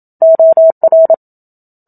Morse code, heard as a keyed tone, about 27 wpm, 650 Hz. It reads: OR
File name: Book Two / 01-Chapter 1 - 27 WPM